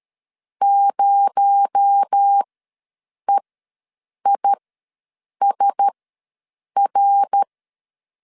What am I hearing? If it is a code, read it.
Morse code, heard as a keyed tone, about 13 wpm, spaced as for 7 wpm, 800 Hz. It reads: 0EISR
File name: LEZIONI / 8 YJ45 / 8 YJ4 ws 12wpm 3249